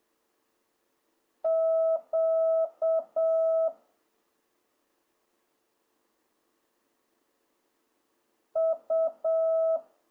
Morse code, heard as a keyed tone, about 7 wpm, 660 Hz. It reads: Q U